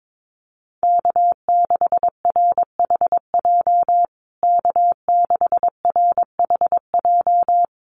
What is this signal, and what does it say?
Morse code, heard as a keyed tone, about 22 wpm, 700 Hz. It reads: K6RHJ K6RHJ
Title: USA Call Signs Repeated 3x 22wpm